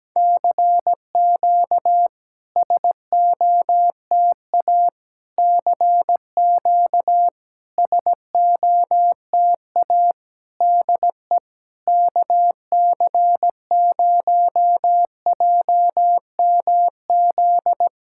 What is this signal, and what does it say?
Morse code, heard as a keyed tone, about 17 wpm, 700 Hz. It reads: CQ SOTA CQ SOTA DE KC0JMZ
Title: SOTA QSO Practice 17wpm